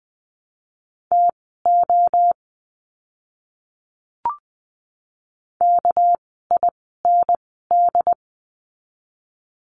Morse code, heard as a keyed tone, about 20 wpm, spaced as for 14 wpm, 700 Hz. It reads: TO KIND
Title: 199 Lesson 32f - Letter k - Words 20wpm